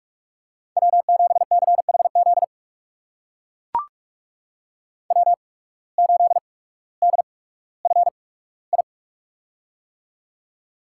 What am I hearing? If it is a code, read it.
Morse code, heard as a keyed tone, about 42 wpm, spaced as for 19 wpm, 700 Hz. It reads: W7CHZ W8DFI
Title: USA Call Signs - 5 Characters - 8x Character Spacing - 45wpm